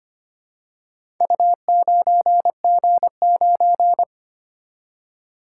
Morse code, heard as a keyed tone, about 25 wpm, 700 Hz. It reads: U9G9
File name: practice / 25wpm / Four Letter-Number Combinations - 25wpm